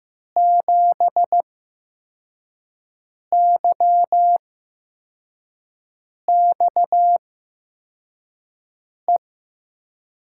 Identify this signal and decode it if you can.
Morse code, heard as a keyed tone, about 15 wpm, spaced as for 4 wpm, 700 Hz. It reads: 7YXE
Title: USA Call Signs - 6 Characters - 8x Character Spacing - 15wpm